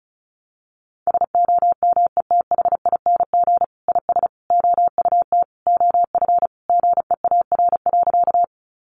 Morse code, heard as a keyed tone, about 34 wpm, 700 Hz. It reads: SOMETHING IS OUT OF GEAR.
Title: Sentences from Top 1400 Words 35wpm